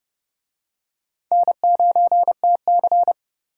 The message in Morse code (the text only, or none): N9TC